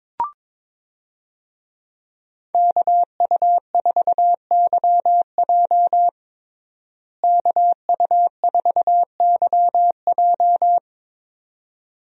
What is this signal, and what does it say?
Morse code, heard as a keyed tone, about 22 wpm, 700 Hz. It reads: KU4YJ KU4YJ
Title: USA Call Signs - 5 Characters - Repeated 2x - 22wpm